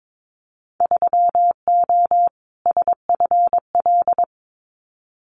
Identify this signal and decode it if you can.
Morse code, heard as a keyed tone, about 22 wpm, 700 Hz. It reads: 3O SFL
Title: Winter Field Day Exchanges 22wpm